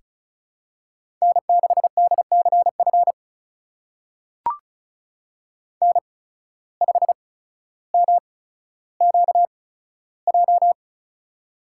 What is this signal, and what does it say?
Morse code, heard as a keyed tone, about 34 wpm, spaced as for 15 wpm, 700 Hz. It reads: N6DCF N5MQJ
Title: USA Call Signs - 5 Characters - 8x Character Spacing - 35wpm